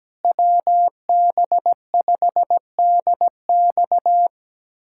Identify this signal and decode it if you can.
Morse code, heard as a keyed tone, about 17 wpm, 700 Hz. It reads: WB5DX
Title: USA Callsign DE Callsign - 5 Characters - 17wpm